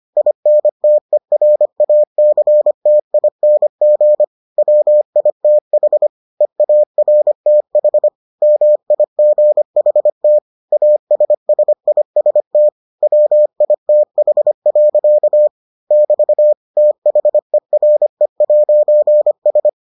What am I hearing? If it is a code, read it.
Morse code, heard as a keyed tone, about 25 wpm, 600 Hz. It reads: INTERACTING WITH EARTH MIGHT ASSIST WITH. <BT> THERE'S